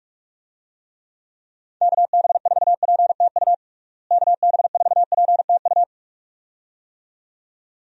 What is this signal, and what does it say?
Morse code, heard as a keyed tone, about 46 wpm, 700 Hz. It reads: KB4PTU KB4PTU
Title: USA Call Signs - 6 Characters - Repeated 2x - 45wpm